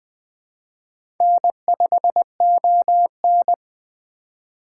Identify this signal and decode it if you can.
Morse code, heard as a keyed tone, about 20 wpm, 700 Hz. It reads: N5ON